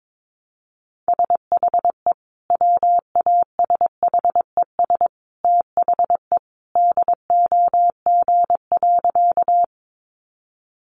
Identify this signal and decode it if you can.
Morse code, heard as a keyed tone, about 22 wpm, 700 Hz. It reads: SHE WASHES THE DOG.
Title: Easy Phrases 22wpm